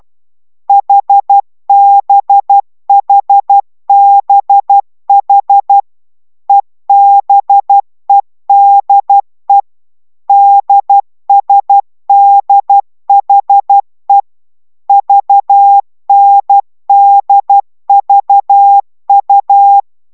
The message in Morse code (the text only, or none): HBHBH EBEDE DSDHE VNDVU